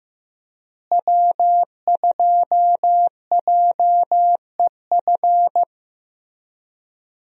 Morse code, heard as a keyed tone, about 15 wpm, 700 Hz. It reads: W2JEF
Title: USA Call Signs - 5 Characters - Repeated 2x - 15wpm